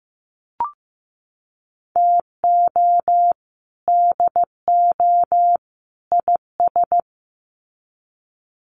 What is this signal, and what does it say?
Morse code, heard as a keyed tone, about 15 wpm, 700 Hz. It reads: TO DO IS
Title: Sets of 3 Words 15wpm